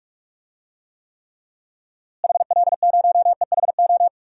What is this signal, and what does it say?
Morse code, heard as a keyed tone, about 44 wpm, 700 Hz. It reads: HL0EHO
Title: DXCC-100 Call Signs - 6 Characters - Repeated 3x - 45wpm